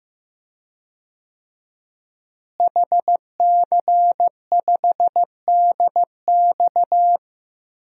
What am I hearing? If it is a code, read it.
Morse code, heard as a keyed tone, about 15 wpm, 700 Hz. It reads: HC5DX
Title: DXCC-100 Call Signs - 5 Characters - Repeated 3x - 15wpm